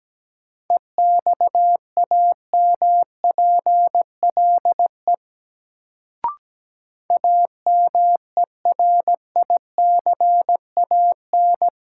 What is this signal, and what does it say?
Morse code, heard as a keyed tone, about 17 wpm, 700 Hz. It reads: EXAMPLE AMERICAN
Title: Sets of 2 Words - New Word Drill - 17wpm